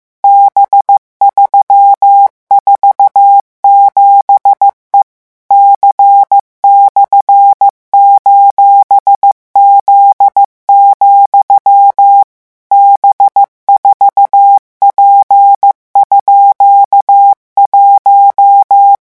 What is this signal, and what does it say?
Morse code, heard as a keyed tone, about 15 wpm, 800 Hz. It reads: B347E C/:Z, B4P_1
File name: morse mp3 / revision17-vitesse 14 mots